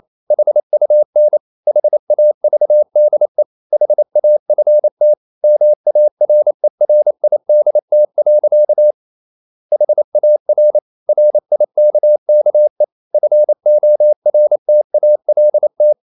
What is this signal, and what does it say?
Morse code, heard as a keyed tone, about 28 wpm, 600 Hz. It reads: HUN HAVDE HAFT MARERIDT. HAR RIKKE FORTALT
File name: Ferie QRQ 28wpm